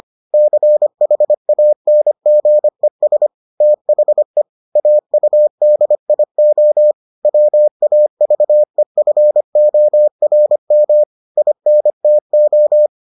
Morse code, heard as a keyed tone, about 25 wpm, 600 Hz. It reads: CHANGES THE AUDIO WAVEFORM INTO